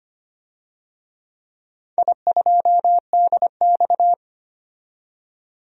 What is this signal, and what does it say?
Morse code, heard as a keyed tone, about 25 wpm, 700 Hz. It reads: I2DX
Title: DXCC-100 Call Signs - 4 Characters - 25wpm